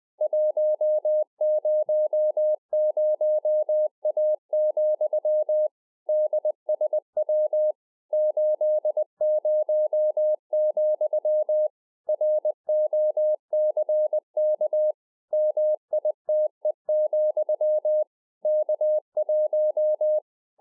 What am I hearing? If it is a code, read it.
Morse code, heard as a keyed tone, about 20 wpm, 610 Hz. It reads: 100A, DSW 80, ROCK MITE, K1
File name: 2011-04-mp3 / A SMALL TRANCEIVER 3 20WPM